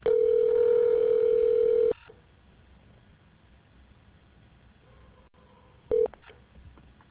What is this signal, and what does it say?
Ringback tone (North America), 440 + 480 Hz.